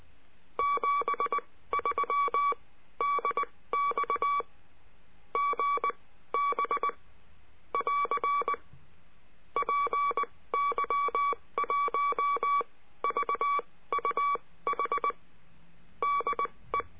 Morse code, heard as a keyed tone, about 19 wpm, spaced as for 15 wpm, 1140 Hz. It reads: 73 DX GB <AR> PY1VUH DE
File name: cursos / CURSO DE TELEGRAFIA / Telegrafia - Trilha 22